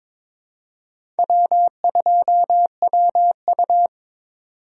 Morse code, heard as a keyed tone, about 22 wpm, 700 Hz. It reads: W2WU
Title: USA Call Signs - 4 Characters - 22wpm